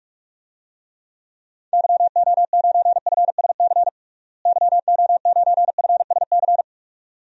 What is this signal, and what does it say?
Morse code, heard as a keyed tone, about 46 wpm, 700 Hz. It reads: YO9FSC YO9FSC
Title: DXCC-100 Call Signs - 6 Characters - Repeated 2x - 45wpm